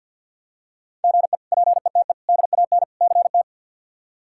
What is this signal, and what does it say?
Morse code, heard as a keyed tone, about 48 wpm, 700 Hz. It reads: GE PETE DAN CT
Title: K1USN SST QSO Practice - 50wpm